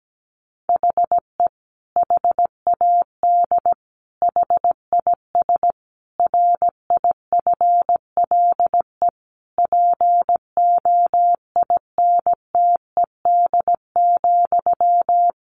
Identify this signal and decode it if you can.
Morse code, heard as a keyed tone, about 17 wpm, 700 Hz. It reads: HE HAD HIS RIFLE POINTED,
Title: A Sound Of Thunder 17wpm